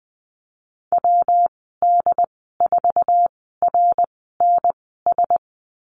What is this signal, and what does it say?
Morse code, heard as a keyed tone, about 20 wpm, spaced as for 14 wpm, 700 Hz. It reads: WD4RNS
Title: 140 Lesson 23g - Letter f - Call Signs 20wpm